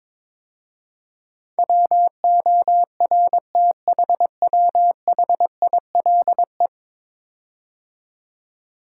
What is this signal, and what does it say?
Morse code, heard as a keyed tone, about 21 wpm, 700 Hz. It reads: WORTHWHILE
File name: practice / 22wpm / 10 Letter Words - First 3 Letters Given - 22wpm